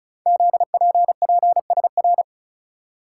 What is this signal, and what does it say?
Morse code, heard as a keyed tone, about 36 wpm, 690 Hz. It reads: ZPPSR
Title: Code Groups - 5 Letters - 35wpm